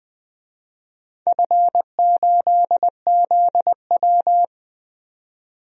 Morse code, heard as a keyed tone, about 20 wpm, 700 Hz. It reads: F8ZW
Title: DXCC-100 Call Signs - 4 Characters - 20wpm